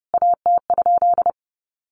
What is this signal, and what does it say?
Morse code, heard as a keyed tone, about 30 wpm, 700 Hz. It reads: AT?
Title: A Sound Of Thunder 30wpm